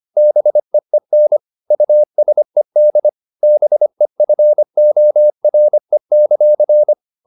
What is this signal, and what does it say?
Morse code, heard as a keyed tone, about 25 wpm, 600 Hz. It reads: BEEN USED BEFORE;
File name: the old man and the sea part 1 25wpm